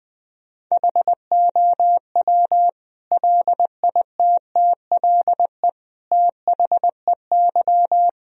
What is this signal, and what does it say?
Morse code, heard as a keyed tone, about 19 wpm, 700 Hz. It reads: HOW LITTLE THEY
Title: Sentences from Top 1000 Words - Review Part 1 - 20wpm